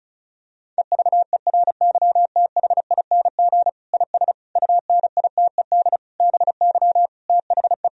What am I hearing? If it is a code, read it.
Morse code, heard as a keyed tone, about 36 wpm, 700 Hz. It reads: EVERYTHING IS UNITED BY THE
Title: Sentences from Top 1000 Words - Review Part 4 - 35wpm